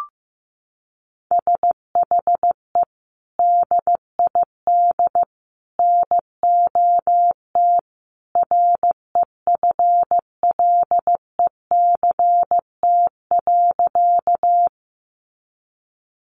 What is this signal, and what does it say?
Morse code, heard as a keyed tone, about 15 wpm, 700 Hz. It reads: SHE DID NOT REFLECT.
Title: Sentences from Top 1500 Words 15wpm